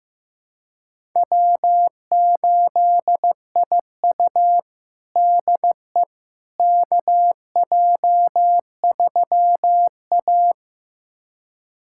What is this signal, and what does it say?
Morse code, heard as a keyed tone, about 15 wpm, 700 Hz. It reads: W8IU DE KJ3A
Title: USA Callsign DE Callsign - 4 Characters - 15wpm